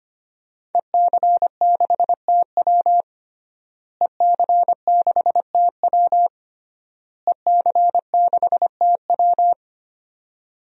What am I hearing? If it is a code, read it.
Morse code, heard as a keyed tone, about 25 wpm, 700 Hz. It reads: EC6TW EC6TW EC6TW
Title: DXCC-100 Call Signs - 5 Characters - Repeated 3x - 25wpm